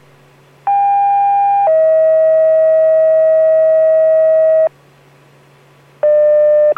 Two-tone page, heard as a one-second tone, then a three-second tone, about 800 then 640 Hz. It is heard from 0.7 to 4.7 s.